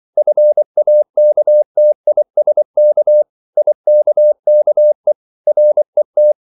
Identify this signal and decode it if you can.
Morse code, heard as a keyed tone, about 24 wpm, 600 Hz. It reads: FAKTISK IKKE RET
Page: Ferie 24 wpm